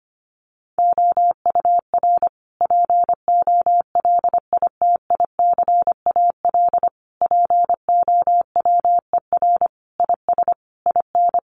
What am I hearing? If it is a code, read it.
Morse code, heard as a keyed tone, about 25 wpm, 700 Hz. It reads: OUR POLITICAL POWER IS IN
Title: Sentences from Top 900 Words 25wpm